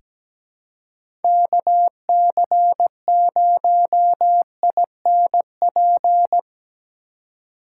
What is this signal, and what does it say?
Morse code, heard as a keyed tone, about 17 wpm, 700 Hz. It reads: KC0INP
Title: K1USN SST QSO Practice - 17wpm